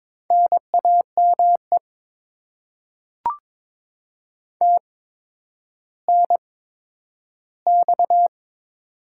Morse code, heard as a keyed tone, about 22 wpm, spaced as for 11 wpm, 700 Hz. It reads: NAME TNX
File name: Top 20 QSO Elements - 8x Character Spacing - 22wpm